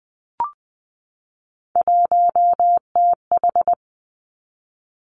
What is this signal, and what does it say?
Morse code, heard as a keyed tone, about 20 wpm, 700 Hz. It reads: 1TH
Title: Three Challenging Letter-Number Combinations 20wpm